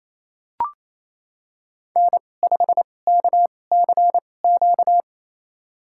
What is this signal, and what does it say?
Morse code, heard as a keyed tone, about 28 wpm, spaced as for 20 wpm, 700 Hz. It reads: N5KCQ
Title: USA Call Signs - 5 Characters - 2x Character Spacing - 28wpm